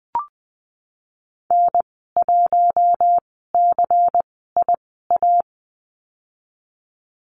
Morse code, heard as a keyed tone, about 20 wpm, spaced as for 15 wpm, 700 Hz. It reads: N1CIA
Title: 173 Lesson 28g - Number 9 - Call Signs 20wpm